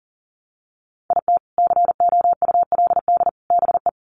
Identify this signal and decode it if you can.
Morse code, heard as a keyed tone, about 40 wpm, 700 Hz. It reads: IT COULD BE
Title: Sets of 3 Words 40wpm